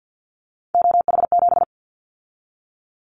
Morse code, heard as a keyed tone, about 50 wpm, 700 Hz. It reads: OH7
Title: Callsign Prefixes for Top 50 DXCC - 50wpm